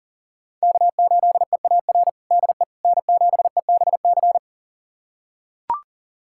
Morse code, heard as a keyed tone, about 40 wpm, 700 Hz. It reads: K8EAR DE N7EBC
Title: USA Callsign DE Callsign - 5 Characters - 40wpm